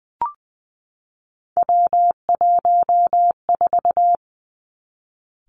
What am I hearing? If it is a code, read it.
Morse code, heard as a keyed tone, about 19 wpm, 700 Hz. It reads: W14
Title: Three Challenging Letter-Number Combinations 20wpm